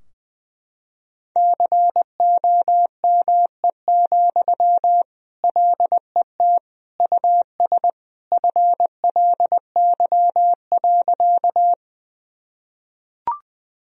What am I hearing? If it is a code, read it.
Morse code, heard as a keyed tone, about 20 wpm, 700 Hz. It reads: COME, LET US FLY.
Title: Sentences from Top 1000 Words - Review Part 2 - 20wpm